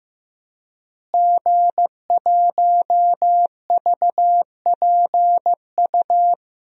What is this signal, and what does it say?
Morse code, heard as a keyed tone, about 15 wpm, 700 Hz. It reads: G1VPU
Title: DXCC-100 Call Signs - 5 Characters - Repeated 3x - 15wpm